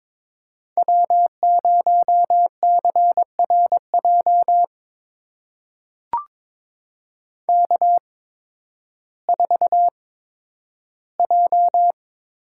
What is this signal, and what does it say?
Morse code, heard as a keyed tone, about 21 wpm, spaced as for 12 wpm, 700 Hz. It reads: W0CRJ K4J
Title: USA Call Signs - 5 Characters - 8x Character Spacing - 22wpm